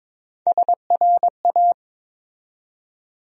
Morse code, heard as a keyed tone, about 23 wpm, 700 Hz. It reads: SRA
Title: USA Call Sign Postfixes - 3 Letters 22wpm